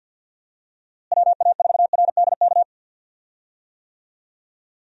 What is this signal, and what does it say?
Morse code, heard as a keyed tone, about 50 wpm, 700 Hz. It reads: WA4RDK
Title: USA Call Signs - 6 Characters - Repeated 3x - 50wpm